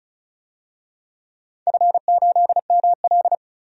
Morse code, heard as a keyed tone, about 34 wpm, 700 Hz. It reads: F8ML
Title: DXCC-100 Call Signs - 4 Characters - 35wpm